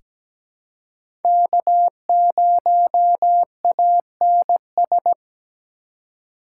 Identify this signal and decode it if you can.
Morse code, heard as a keyed tone, about 17 wpm, 700 Hz. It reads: K0ANS
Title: USA Call Signs - 5 Characters - 3x Character Spacing - 17wpm